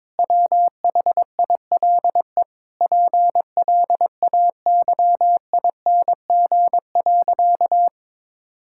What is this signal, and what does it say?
Morse code, heard as a keyed tone, about 22 wpm, 700 Hz. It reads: WHILE PLAYING.
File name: Difficult Sentences with Hints - 22wpm